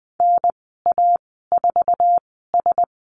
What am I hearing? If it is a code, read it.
Morse code, heard as a keyed tone, about 20 wpm, spaced as for 15 wpm, 700 Hz. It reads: NA4S
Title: 055 Lesson 10g - Letter r - Call Signs 20wpm